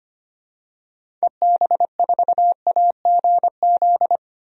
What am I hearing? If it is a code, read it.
Morse code, heard as a keyed tone, about 25 wpm, 700 Hz. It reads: EB4AGZ
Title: DXCC-100 Call Signs - 6 Characters - Repeated 3x - 25wpm